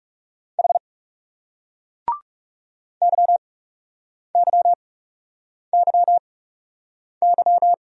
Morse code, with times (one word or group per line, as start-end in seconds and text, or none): 0.6-0.8 H
3.0-7.7 YYYY